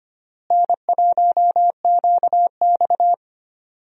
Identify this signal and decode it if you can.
Morse code, heard as a keyed tone, about 25 wpm, 700 Hz. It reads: N1QX
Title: USA Call Signs - 4 Characters - Repeated 3x - 25wpm